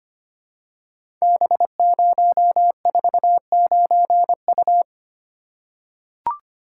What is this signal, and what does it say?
Morse code, heard as a keyed tone, about 25 wpm, 700 Hz. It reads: B049U
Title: Code Groups - 5 Letter:Numbers - 25wpm